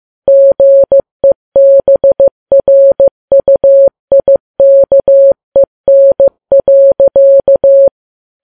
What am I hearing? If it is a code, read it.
Morse code, heard as a keyed tone, about 15 wpm, 550 Hz. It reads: GEBRUIKEN.
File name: Files Web tls / Morse on9fv / morse15kps